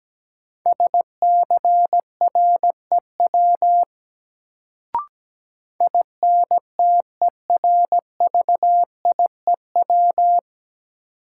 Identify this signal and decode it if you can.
Morse code, heard as a keyed tone, about 17 wpm, 700 Hz. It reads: SCREW INTERVIEW
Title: Sentences from Top 1600 Words - New Word Drill - 17wpm